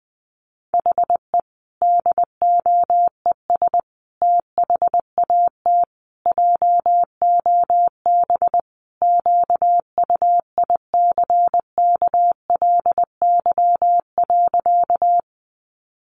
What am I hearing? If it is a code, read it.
Morse code, heard as a keyed tone, about 20 wpm, 700 Hz. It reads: HE DOES THAT JOB QUICKLY.
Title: Sentences from Top 700 Words 20wpm